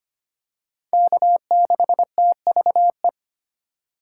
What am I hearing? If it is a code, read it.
Morse code, heard as a keyed tone, about 25 wpm, 700 Hz. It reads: K6TVE
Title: USA Call Signs - 4x Character Spacing - 25wpm